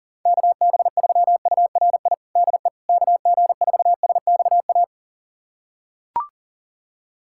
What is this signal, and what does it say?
Morse code, heard as a keyed tone, about 40 wpm, 700 Hz. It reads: KB3URI DE KG4SXA
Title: USA Callsign DE Callsign 40wpm